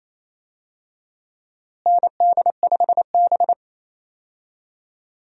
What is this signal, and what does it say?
Morse code, heard as a keyed tone, about 28 wpm, 700 Hz. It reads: ND5B
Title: USA Call Signs - 4 Characters - Repeated 3x - 28wpm